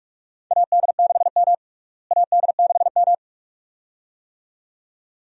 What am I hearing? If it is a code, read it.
Morse code, heard as a keyed tone, about 46 wpm, 700 Hz. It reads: AD6M AD6M
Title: USA Call Signs - 4 Characters - Repeated 2x - 45wpm